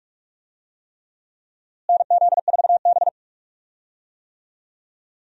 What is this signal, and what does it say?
Morse code, heard as a keyed tone, about 45 wpm, 700 Hz. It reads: NZ4B
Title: USA Call Signs - 4 Characters - Repeated 3x - 45wpm